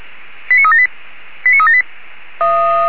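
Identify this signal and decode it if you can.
Five-tone selective call (ZVEI), heard as quick runs of five tones, tones of about 70 ms. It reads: 89378 89378
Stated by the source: ZVEI five-tone call received on an SDR.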